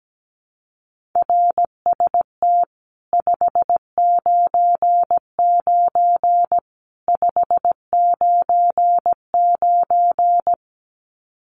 Morse code, heard as a keyed tone, about 17 wpm, 700 Hz. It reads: RST 599 599
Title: Common QSO Phrases 17wpm